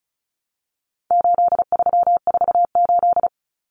Morse code, heard as a keyed tone, about 34 wpm, 700 Hz. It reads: 8348
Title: Common Cut Numbers - 4-Digit 35wpm